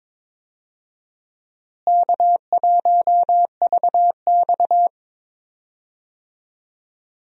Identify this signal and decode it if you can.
Morse code, heard as a keyed tone, about 22 wpm, 700 Hz. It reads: K1VX
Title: USA Call Signs - 4 Characters - Repeated 3x - 22wpm